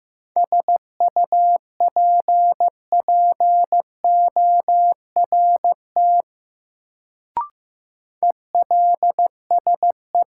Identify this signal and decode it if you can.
Morse code, heard as a keyed tone, about 15 wpm, 700 Hz. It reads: SUPPORT ELSE
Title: Sentences from Top 800 Words - New Word Drill - 15wpm